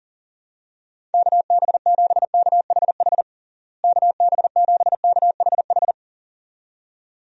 Morse code, heard as a keyed tone, about 40 wpm, 700 Hz. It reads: KB7KHH KB7KHH
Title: USA Call Signs - 6 Characters - Repeated 2x - 40wpm